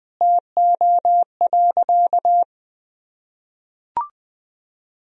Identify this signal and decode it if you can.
Morse code, heard as a keyed tone, about 20 wpm, 700 Hz. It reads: TO.
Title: Sentences from Top 1000 Words - Review Part 2 - 20wpm